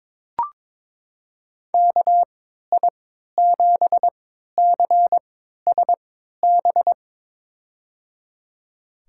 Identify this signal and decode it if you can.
Morse code, heard as a keyed tone, about 22 wpm, spaced as for 13 wpm, 700 Hz. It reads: KI7CSB
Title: USA Call Signs - 6 Characters - 3x Character Spacing - 22wpm